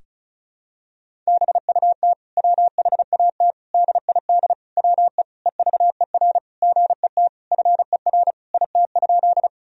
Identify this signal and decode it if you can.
Morse code, heard as a keyed tone, about 36 wpm, 700 Hz. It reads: BUT WHAT DID WE EVER GET FER IT?
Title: Sentences from Top 400 Words and Abbreviations 35wpm